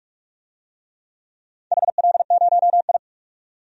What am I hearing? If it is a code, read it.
Morse code, heard as a keyed tone, about 44 wpm, 700 Hz. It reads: HL0I
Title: DXCC-100 Call Signs - 4 Characters - 45wpm